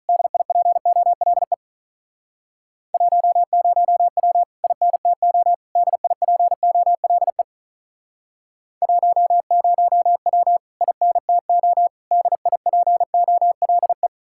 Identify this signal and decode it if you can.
Morse code, heard as a keyed tone, about 45 wpm, 700 Hz. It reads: DIPOLE 10W INTO DIPOLE 10W INTO DIPOLE